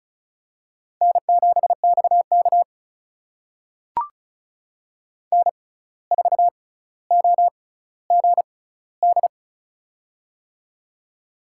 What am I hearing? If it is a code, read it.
Morse code, heard as a keyed tone, about 36 wpm, spaced as for 10 wpm, 700 Hz. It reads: N7X KN4OGD
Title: USA Call Signs - 6x Character Spacing - 35wpm